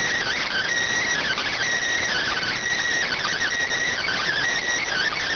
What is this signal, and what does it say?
Emergency Alert System (SAME) tones, heard as harsh buzzing bursts. It is heard from 0.2 to 4.3 s.